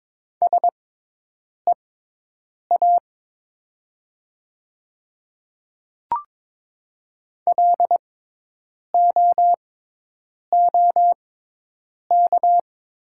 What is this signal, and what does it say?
Morse code, heard as a keyed tone, about 22 wpm, spaced as for 8 wpm, 700 Hz. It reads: SEA LOOK
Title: 6x Character Spacing - Top 300 Words - 22wpm